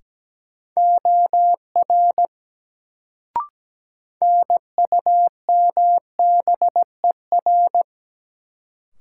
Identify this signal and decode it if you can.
Morse code, heard as a keyed tone, about 17 wpm, 700 Hz. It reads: OR NUMBER
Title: Sentences from Top 100 Words - New Word Drill - 17wpm